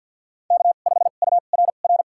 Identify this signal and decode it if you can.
Morse code, heard as a keyed tone, about 50 wpm, spaced as for 36 wpm, 700 Hz. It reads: K5URR
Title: USA Call Signs - 5 Characters - 2x Character Spacing - 50wpm